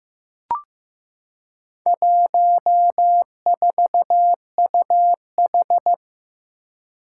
Morse code, heard as a keyed tone, about 15 wpm, 700 Hz. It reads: 14UH
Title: Four Challenging Letter-Number Combinations 15wpm